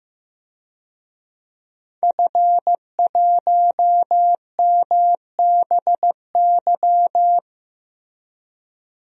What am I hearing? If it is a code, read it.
Morse code, heard as a keyed tone, about 15 wpm, 700 Hz. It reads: F1MBY
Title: DXCC-100 Call Signs - 5 Characters - Repeated 3x - 15wpm